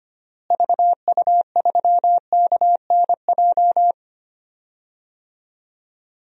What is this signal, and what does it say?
Morse code, heard as a keyed tone, about 25 wpm, 700 Hz. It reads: VU3KNJ
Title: DXCC-100 Call Signs - 6 Characters - Repeated 2x - 25wpm